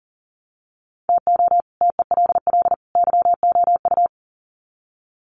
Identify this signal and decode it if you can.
Morse code, heard as a keyed tone, about 39 wpm, 700 Hz. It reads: TO TELL YOU
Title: Sets of 3 Words 40wpm